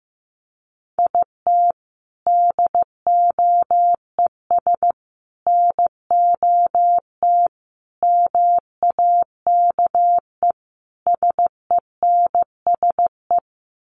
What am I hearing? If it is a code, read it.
Morse code, heard as a keyed tone, about 15 wpm, 700 Hz. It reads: IT DOES NOT MAKE SENSE
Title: Sets of 5 Words Encore2 15wpm